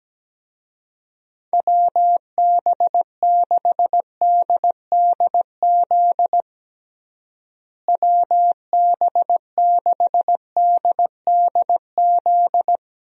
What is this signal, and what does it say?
Morse code, heard as a keyed tone, about 17 wpm, 700 Hz. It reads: WB6DDZ WB6DDZ
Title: USA Call Signs - 6 Characters - Repeated 3x - 17wpm